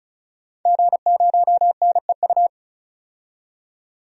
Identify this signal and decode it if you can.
Morse code, heard as a keyed tone, about 36 wpm, 700 Hz. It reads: G0NEU
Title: DXCC-100 Call Signs - 5 Characters - 35wpm